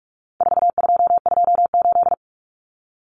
Morse code, heard as a keyed tone, about 44 wpm, 700 Hz. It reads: 4228
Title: Numbers - 4-Digit 45wpm